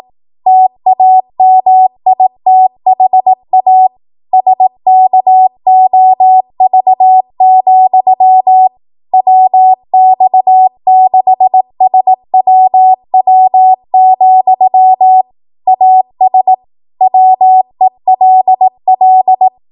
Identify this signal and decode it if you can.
Morse code, heard as a keyed tone, about 18 wpm, 750 Hz. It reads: TAMITHA SKOV, WX6SWW, AS WELL
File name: Archive / 18 WPM / 18WPM